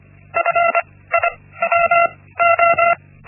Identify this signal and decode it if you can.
Morse code, heard as a keyed tone, about 25 wpm, 680 Hz. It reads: F I W O
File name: Linksmakalnis-1